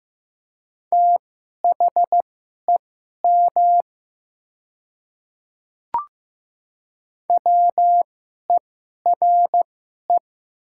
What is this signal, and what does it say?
Morse code, heard as a keyed tone, about 15 wpm, spaced as for 11 wpm, 700 Hz. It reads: THEM WERE
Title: 2x Character Spacing - Top 300 Words - 15wpm